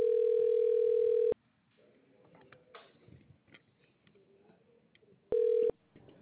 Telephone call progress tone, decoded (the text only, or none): ringback tone